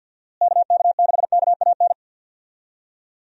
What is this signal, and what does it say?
Morse code, heard as a keyed tone, about 52 wpm, 700 Hz. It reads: KK6KAN
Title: USA Call Signs - 6 Characters - Repeated 2x - 50wpm